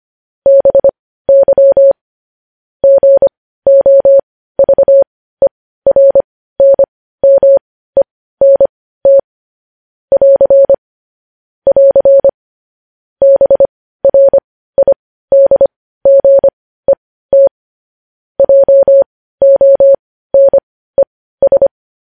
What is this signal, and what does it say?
Morse code, heard as a keyed tone, about 25 wpm, spaced as for 15 wpm, 550 Hz. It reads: BY GOVERNMENT <AR> <AR> BRIDGET JONES
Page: News Headlines in Morse Code at 15 WPM